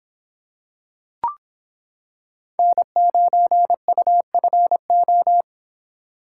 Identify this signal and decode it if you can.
Morse code, heard as a keyed tone, about 26 wpm, 700 Hz. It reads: N9UFO